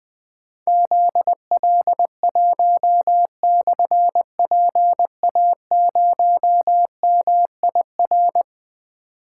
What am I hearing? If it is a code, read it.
Morse code, heard as a keyed tone, about 20 wpm, 700 Hz. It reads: ZL1/PA0MIR
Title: Difficult Call Signs 20wpm